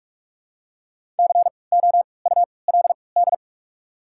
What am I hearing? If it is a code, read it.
Morse code, heard as a keyed tone, about 46 wpm, spaced as for 25 wpm, 690 Hz. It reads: COULD
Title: 3x Character Spacing - Top 100 Words - 45wpm